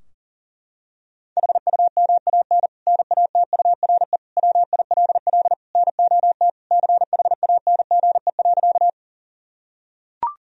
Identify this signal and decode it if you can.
Morse code, heard as a keyed tone, about 37 wpm, 700 Hz. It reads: HUMAN NATURE WILL NOT CHANGE.
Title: Sentences from Top 1000 Words - Review Part 3 - 40wpm